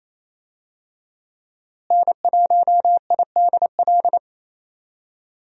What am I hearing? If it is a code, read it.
Morse code, heard as a keyed tone, about 27 wpm, 700 Hz. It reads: N1IDL
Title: USA Call Signs - 5 Characters - Repeated 2x - 28wpm